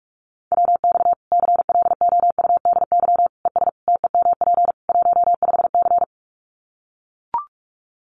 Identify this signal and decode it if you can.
Morse code, heard as a keyed tone, about 44 wpm, 700 Hz. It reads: WX CLOUDY ES TEMP 15C
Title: Common QSO Phrases 45wpm